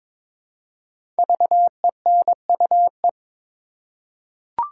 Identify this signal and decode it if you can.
Morse code, heard as a keyed tone, about 22 wpm, 700 Hz. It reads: VENUE
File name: 5 Letter Words - First 3 Letters Given - 22wpm